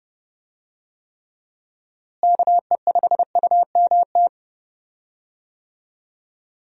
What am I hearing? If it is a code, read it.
Morse code, heard as a keyed tone, about 30 wpm, 700 Hz. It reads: KE5UMT
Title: USA Call Signs - 6 Characters - Repeated 2x - 30wpm